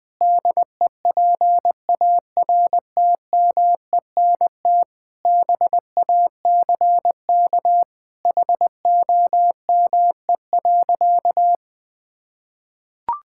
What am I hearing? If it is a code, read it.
Morse code, heard as a keyed tone, about 19 wpm, 700 Hz. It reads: DEPARTMENT BACK HOME.
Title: Sentences from Top 1000 Words - Review Part 6 - 20wpm